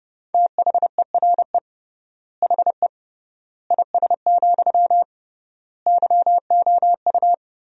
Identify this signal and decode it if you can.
Morse code, heard as a keyed tone, about 30 wpm, 700 Hz. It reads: THERE HE IS, YOU
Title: 3x Word-Spacing - Sentences from Top 100 Words - 30wpm